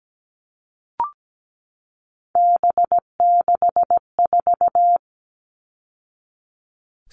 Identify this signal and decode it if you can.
Morse code, heard as a keyed tone, about 17 wpm, 700 Hz. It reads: B64
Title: All Cut Numbers - 3-Digit 17wpm